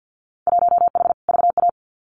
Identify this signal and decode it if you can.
Morse code, heard as a keyed tone, about 50 wpm, 700 Hz. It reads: 1H VA
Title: Winter Field Day Exchanges 50wpm